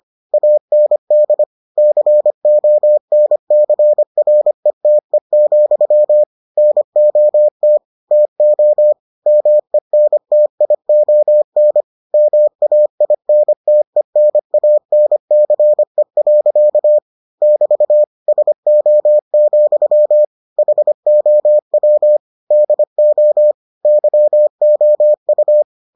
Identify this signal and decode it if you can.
Morse code, heard as a keyed tone, about 25 wpm, 600 Hz. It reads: AND CONCRETE, NOT TO MENTION MAINTENANCE. <BT> SO, HOW DO YOU